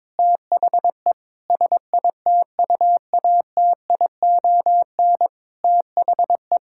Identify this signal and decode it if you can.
Morse code, heard as a keyed tone, about 21 wpm, 700 Hz. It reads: THE SITUATION THE
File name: Sentences with 9-Word Phrases - 22wpm